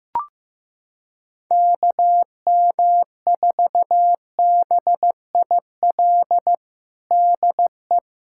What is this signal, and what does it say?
Morse code, heard as a keyed tone, about 15 wpm, 700 Hz. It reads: KM4BIL DE
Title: USA Callsign DE Callsign - 6 Characters - 15wpm